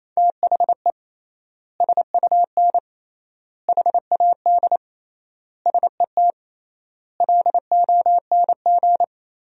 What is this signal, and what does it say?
Morse code, heard as a keyed tone, about 28 wpm, 700 Hz. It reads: THE SUN HAD SET LONG
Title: 3x Word-Spacing - Sentences from Top 300 Words - 28wpm